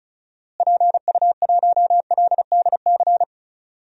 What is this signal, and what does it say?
Morse code, heard as a keyed tone, about 34 wpm, 700 Hz. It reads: PU1LDC
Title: DXCC-100 Call Signs - 6 Characters - Repeated 2x - 35wpm